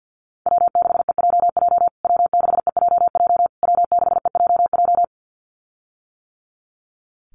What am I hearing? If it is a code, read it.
Morse code, heard as a keyed tone, about 49 wpm, 700 Hz. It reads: W6EJJ W6EJJ W6EJJ
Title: USA Call Signs Repeated 3x 50wpm